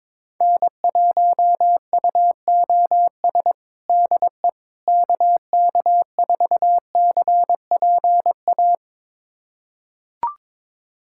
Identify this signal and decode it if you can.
Morse code, heard as a keyed tone, about 22 wpm, 700 Hz. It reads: N1UOS DE KK4CPA
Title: USA Callsign DE Callsign 22wpm